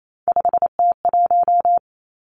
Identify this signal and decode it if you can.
Morse code, heard as a keyed tone, about 28 wpm, 700 Hz. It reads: 5T1
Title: Three Challenging Letter-Number Combinations 28wpm